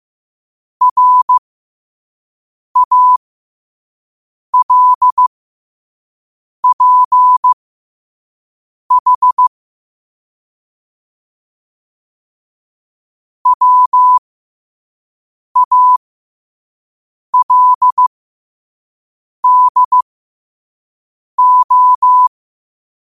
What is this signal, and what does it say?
Morse code, heard as a keyed tone, about 15 wpm, spaced as for 5 wpm, 1000 Hz. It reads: RALPH WALDO
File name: qotdfiles / 5wpm